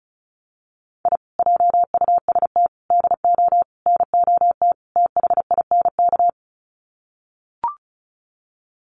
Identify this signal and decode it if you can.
Morse code, heard as a keyed tone, about 34 wpm, 700 Hz. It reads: I JUST DO NOT THINK
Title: Sets of 5 Words Encore 35wpm